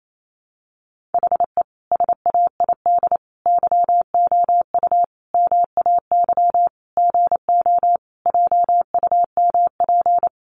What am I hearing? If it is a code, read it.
Morse code, heard as a keyed tone, about 28 wpm, 700 Hz. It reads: HE SAID YOU MAY GO JUMP
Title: Sentences from Top 1300 Words 28wpm